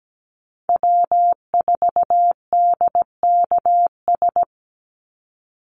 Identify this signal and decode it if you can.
Morse code, heard as a keyed tone, about 17 wpm, 700 Hz. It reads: W4DKS
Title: Call Signs - 17wpm